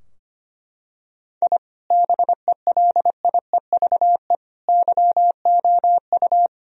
Morse code, heard as a keyed tone, about 25 wpm, 700 Hz. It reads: I BELIEVE YOU
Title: Sentences from Top 1000 Words - Review Part 3 - 25wpm